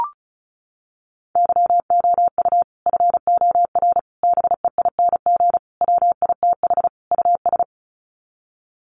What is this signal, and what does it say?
Morse code, heard as a keyed tone, about 34 wpm, 700 Hz. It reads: YOU FOR BEING WITH US
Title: Sets of 5 Words 35wpm